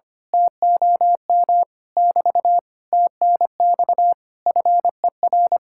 Morse code, heard as a keyed tone, about 24 wpm, 700 Hz. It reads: TOM <BT> TNX FER
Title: Speed-Racing Common QSO Phrases - 25wpm